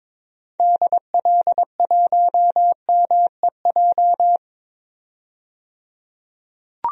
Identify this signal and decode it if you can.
Morse code, heard as a keyed tone, about 22 wpm, 700 Hz. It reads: DL1MEJ
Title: DXCC-100 Call Signs - 6 Characters - 22wpm